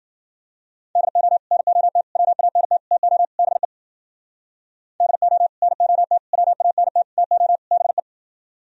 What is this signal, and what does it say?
Morse code, heard as a keyed tone, about 59 wpm, 700 Hz. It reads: DO NOT WANT TO BE DO NOT WANT TO BE